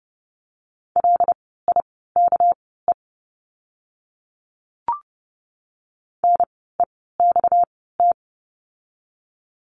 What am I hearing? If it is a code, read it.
Morse code, heard as a keyed tone, about 30 wpm, spaced as for 17 wpm, 700 Hz. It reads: LIKE NEXT
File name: course / 30:10wpm / 237 Lesson 38f - Letter q - Words 30wpm